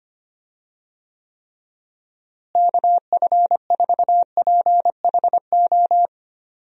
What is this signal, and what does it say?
Morse code, heard as a keyed tone, about 25 wpm, 700 Hz. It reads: KF4PHO
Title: USA Call Signs - 6 Characters - Repeated 3x - 25wpm